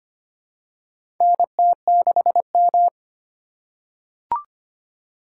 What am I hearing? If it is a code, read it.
Morse code, heard as a keyed tone, about 25 wpm, 700 Hz. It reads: NT6M
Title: USA Call Signs - 3x Character Spacing - 25wpm